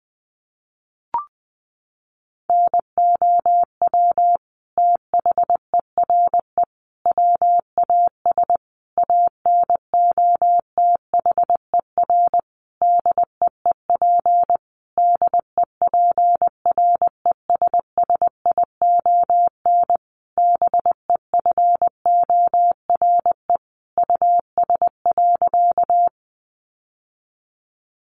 Morse code, heard as a keyed tone, about 20 wpm, 700 Hz. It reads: NOW THERE WAS ANOTHER DEEP DEPRESSION BEFORE US.
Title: Sentences from Top 1300 Words 20wpm